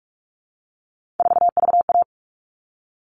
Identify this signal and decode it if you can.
Morse code, heard as a keyed tone, about 44 wpm, 700 Hz. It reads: 4VA
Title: All Cut Numbers - 3-Digit 45wpm